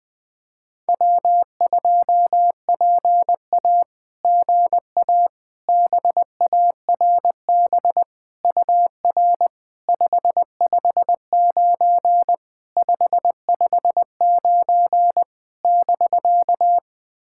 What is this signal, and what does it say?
Morse code, heard as a keyed tone, about 20 wpm, 700 Hz. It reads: W2PA GA BARB UR 559 559 <BK>
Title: SOTA QSO Practice 20wpm